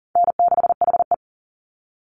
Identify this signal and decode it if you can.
Morse code, heard as a keyed tone, about 40 wpm, 700 Hz. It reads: N6HE
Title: Easy Call Signs - 40wpm